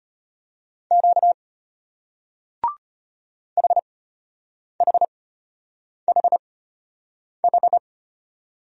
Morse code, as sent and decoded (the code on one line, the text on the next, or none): --.- / .... .... .... ....
Q HHHH